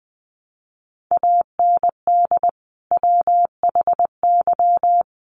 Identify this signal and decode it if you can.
Morse code, heard as a keyed tone, about 20 wpm, 700 Hz. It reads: AND WHY
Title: Sentences from Top 600 Words 20wpm